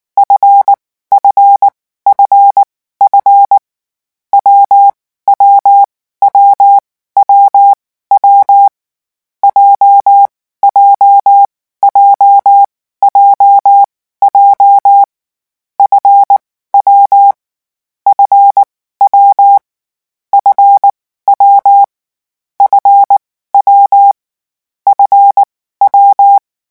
Morse code, heard as a keyed tone, about 19 wpm, 800 Hz. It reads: F F F F W W W W W J J J J J F W F W F W F W F W
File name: morse mp3 / lecon7-vitesse 18